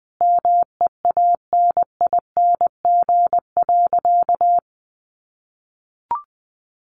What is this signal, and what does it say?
Morse code, heard as a keyed tone, about 20 wpm, 700 Hz. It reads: MEANING.
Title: Sentences from Top 1400 Words 20wpm